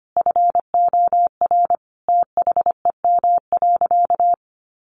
Morse code, heard as a keyed tone, about 25 wpm, 700 Hz. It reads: FOR THEM.
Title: Sentences from Top 900 Words 25wpm